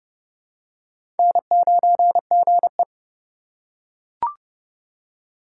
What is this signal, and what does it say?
Morse code, heard as a keyed tone, about 30 wpm, 700 Hz. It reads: N9GE
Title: USA Call Signs - 4 Characters - 3x Character Spacing - 30wpm